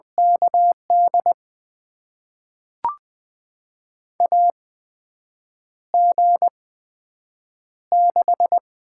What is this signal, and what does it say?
Morse code, heard as a keyed tone, about 20 wpm, spaced as for 7 wpm, 700 Hz. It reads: KD AG6
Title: USA Call Signs - 8x Character Spacing - 20wpm